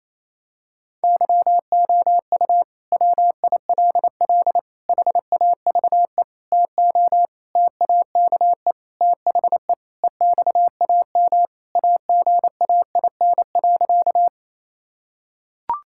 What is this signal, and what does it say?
Morse code, heard as a keyed tone, about 28 wpm, 700 Hz. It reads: YOU WILL HAVE TO TAKE THE EXAM AGAIN.
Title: Sentences from Top 1000 Words - Review Part 5 - 28wpm